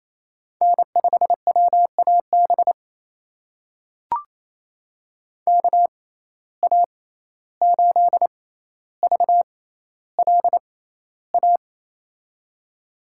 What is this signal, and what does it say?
Morse code, heard as a keyed tone, about 28 wpm, spaced as for 14 wpm, 700 Hz. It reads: N5WAB KA8VLA
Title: USA Call Signs - 6x Character Spacing - 28wpm